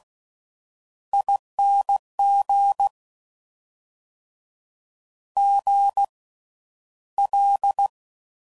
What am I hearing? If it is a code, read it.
Morse code, heard as a keyed tone, about 16 wpm, spaced as for 11 wpm, 800 Hz. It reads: ING GL
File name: Suffixes / Special INGs / Slow With Faster Ings